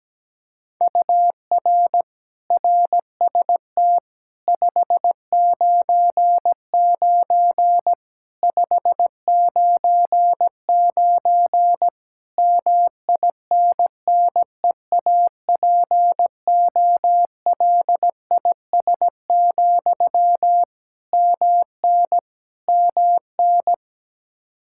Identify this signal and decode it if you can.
Morse code, heard as a keyed tone, about 17 wpm, 700 Hz. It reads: UR RST 599 599 MINNEAPOLIS, MN MN
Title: Practice QSO - Exchange 1 - Beginning - 17wpm